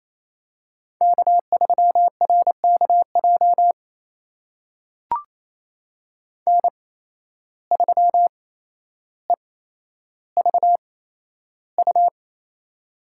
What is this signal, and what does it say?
Morse code, heard as a keyed tone, about 28 wpm, spaced as for 12 wpm, 700 Hz. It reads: K3RKJ N3EVU
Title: USA Call Signs - 5 Characters - 8x Character Spacing - 28wpm